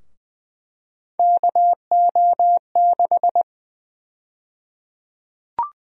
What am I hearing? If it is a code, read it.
Morse code, heard as a keyed tone, about 20 wpm, 700 Hz. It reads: KO6
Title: USA Call Sign Prefixes - Repeated 2x - 20wpm